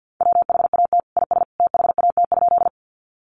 Morse code, heard as a keyed tone, about 48 wpm, 700 Hz. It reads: WHAT IS THAT?
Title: Sentences from Top 100 Words 50wpm